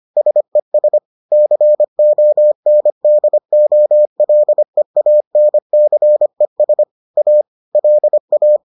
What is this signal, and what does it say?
Morse code, heard as a keyed tone, about 25 wpm, 600 Hz. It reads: SES CONDOLEANCES A LA
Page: bulletin UFT sous forme texte et en CW